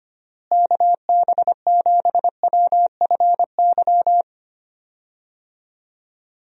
Morse code, heard as a keyed tone, about 25 wpm, 700 Hz. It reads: KB7WFY
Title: USA Call Signs - 6 Characters - Repeated 2x - 25wpm